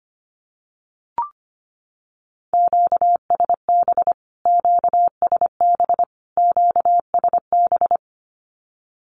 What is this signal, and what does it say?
Morse code, heard as a keyed tone, about 25 wpm, 700 Hz. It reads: QSB QSB QSB
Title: Q-Codes Repeated 3x 25wpm